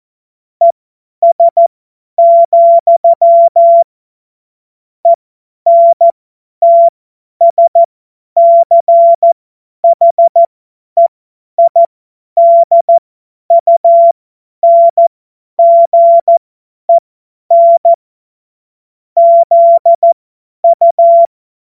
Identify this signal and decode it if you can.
Morse code, heard as a keyed tone, about 14 wpm, spaced as for 10 wpm, 680 Hz. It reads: ES, ENTSCHEIDUNGEN ZU
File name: Morsekurs / Witze / Witz 23. Dezember 2025 50 F70